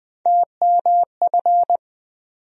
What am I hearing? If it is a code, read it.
Morse code, heard as a keyed tone, about 20 wpm, 700 Hz. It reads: TMF
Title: USA Call Sign Postfixes - 3 Letters 20wpm